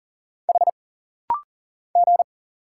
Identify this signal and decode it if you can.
Morse code, heard as a keyed tone, about 40 wpm, 700 Hz. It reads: H G
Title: Single Letter-Number Alt - Rapid-Fire 40wpm